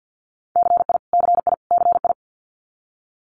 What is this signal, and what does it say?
Morse code, heard as a keyed tone, about 50 wpm, 700 Hz. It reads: CI CI CI
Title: Multi-letter Phonemes Repeated 3x 50wpm